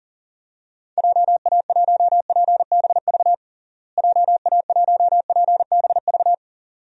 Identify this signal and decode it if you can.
Morse code, heard as a keyed tone, about 40 wpm, 700 Hz. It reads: JA1PBV JA1PBV
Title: DXCC-100 Call Signs - 6 Characters - Repeated 2x - 40wpm